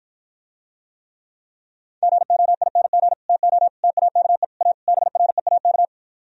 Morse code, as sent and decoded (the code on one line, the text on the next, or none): --. --- .. -. --. / - --- / - .- -.- . / .- / -... .-. .- -.-
GOING TO TAKE A BREAK